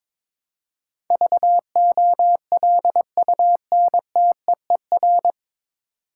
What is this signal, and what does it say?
Morse code, heard as a keyed tone, about 22 wpm, 700 Hz. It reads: VOLUNTEER
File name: 9 Letter Words - First 3 Letters Given - 22wpm